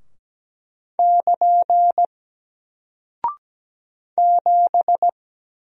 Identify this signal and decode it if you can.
Morse code, heard as a keyed tone, about 17 wpm, 700 Hz. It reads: <KN> 7
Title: Single Character (Letter:Number:Punctuation:Prosign) 17wpm